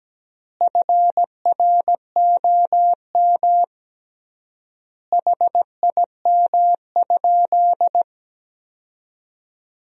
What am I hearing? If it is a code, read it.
Morse code, heard as a keyed tone, about 17 wpm, 700 Hz. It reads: FROM HIM?
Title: 3x Word-Spacing - Sentences from Top 300 Words - 17wpm